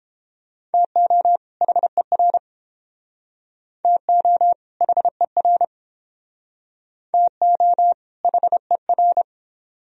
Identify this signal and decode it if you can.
Morse code, heard as a keyed tone, about 30 wpm, 700 Hz. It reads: TO HER TO HER TO HER